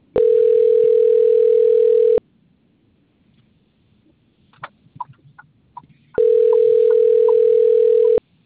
Ringback tone (North America), 440 + 480 Hz.
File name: out- - - - - .16011